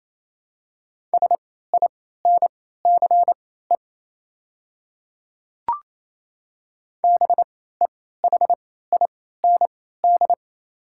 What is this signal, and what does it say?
Morse code, heard as a keyed tone, about 28 wpm, spaced as for 16 wpm, 700 Hz. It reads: SINCE BEHIND
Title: 3x Character Spacing - Top 500 Words - 28wpm